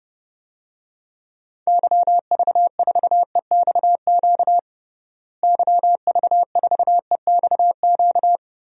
Morse code, heard as a keyed tone, about 30 wpm, 700 Hz. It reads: YV4EXQ YV4EXQ
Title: DXCC-100 Call Signs - 6 Characters - Repeated 2x - 30wpm